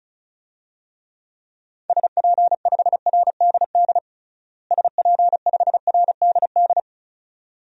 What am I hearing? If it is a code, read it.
Morse code, heard as a keyed tone, about 36 wpm, 700 Hz. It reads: SP5RDD SP5RDD
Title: DXCC-100 Call Signs - 6 Characters - Repeated 2x - 35wpm